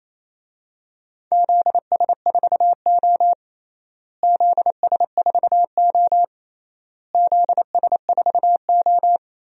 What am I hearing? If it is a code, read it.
Morse code, heard as a keyed tone, about 28 wpm, 700 Hz. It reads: ZS4O ZS4O ZS4O
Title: DXCC-100 Call Signs - 4 Characters - Repeated 3x - 28wpm